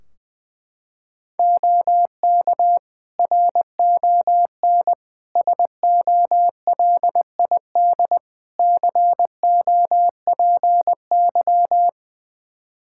Morse code, heard as a keyed tone, about 20 wpm, 700 Hz. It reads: OK RON SOLID COPY
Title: Practice QSO - Exchange 3 - More Info - 20wpm